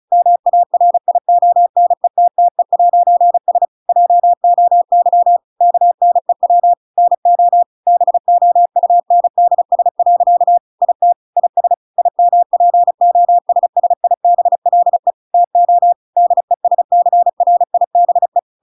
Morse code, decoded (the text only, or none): MARIONETTE'S JOY KNEW NO BOUNDS. IT IS IMPOSSIBLE TO DESCRIBE